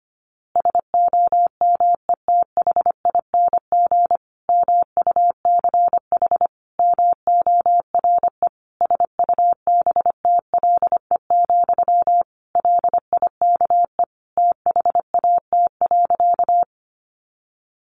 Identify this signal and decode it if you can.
Morse code, heard as a keyed tone, about 25 wpm, 700 Hz. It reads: SOMETHING MUCH MORE SUBTLE, LIKE THAT.
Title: A Sound Of Thunder 25wpm